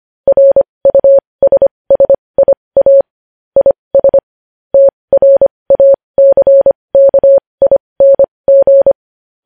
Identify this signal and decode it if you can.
Morse code, heard as a keyed tone, about 25 wpm, spaced as for 20 wpm, 550 Hz. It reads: RUSSIA IS TRACKING
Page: News Headlines in Morse Code at 20 WPM